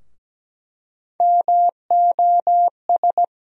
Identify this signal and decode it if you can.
Morse code, heard as a keyed tone, about 17 wpm, 700 Hz. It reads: MOS
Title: USA Call Sign Postfixes - 3 Letters 17wpm